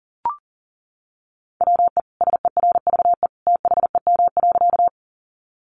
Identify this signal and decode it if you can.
Morse code, heard as a keyed tone, about 40 wpm, 700 Hz. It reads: WE SERVE THEM.
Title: Sentences from Top 400 Words 40wpm